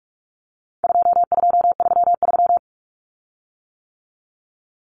Morse code, heard as a keyed tone, about 44 wpm, 700 Hz. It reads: 2233
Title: Common Cut Numbers - 4-Digit 45wpm